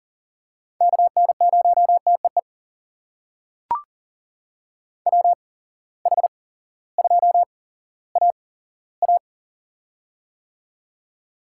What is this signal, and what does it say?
Morse code, heard as a keyed tone, about 38 wpm, spaced as for 18 wpm, 700 Hz. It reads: KN0TEE WH2AA
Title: USA Call Signs - 8x Character Spacing - 40wpm